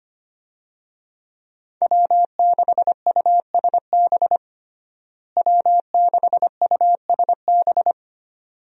Morse code, heard as a keyed tone, about 25 wpm, 700 Hz. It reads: W6USB W6USB
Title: USA Call Signs - 5 Characters - Repeated 2x - 25wpm